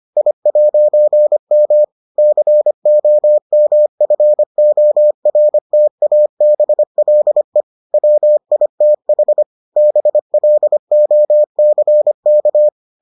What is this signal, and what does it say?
Morse code, heard as a keyed tone, about 25 wpm, 600 Hz. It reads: I'M COMFORTABLE WITH BLOCK